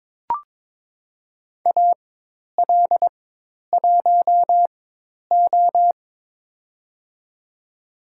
Morse code, heard as a keyed tone, about 22 wpm, spaced as for 10 wpm, 700 Hz. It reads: AL1O
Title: USA Call Signs - 4 Characters - 4x Character Spacing - 22wpm